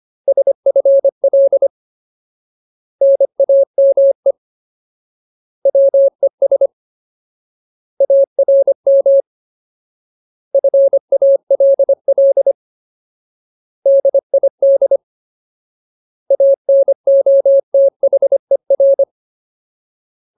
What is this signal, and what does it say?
Morse code, heard as a keyed tone, about 25 wpm, 550 Hz. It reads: SFL NAME WES ARM FALL DID ANOTHER
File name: Ultimate Word List / 25WPM Ultimate